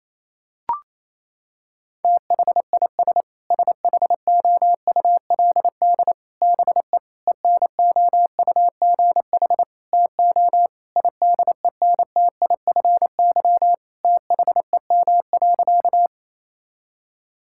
Morse code, heard as a keyed tone, about 28 wpm, 700 Hz. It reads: THIS SHOULD BE ENOUGH TO IDENTIFY THEM.
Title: Sentences from Top 1000 Words - Review Part 6 - 28wpm